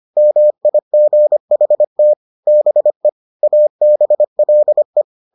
Morse code, heard as a keyed tone, about 25 wpm, 600 Hz. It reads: MIGHT BE ABLE